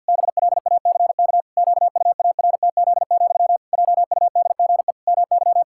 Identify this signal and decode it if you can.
Morse code, heard as a keyed tone, about 50 wpm, 700 Hz. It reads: BLACK QUARTZ, JUDGE MY
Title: Pangram Challenge v2 - 50wpm